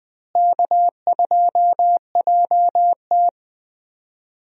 Morse code, heard as a keyed tone, about 20 wpm, 700 Hz. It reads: K2JT
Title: USA Call Signs - 4 Characters - Repeated 3x - 20wpm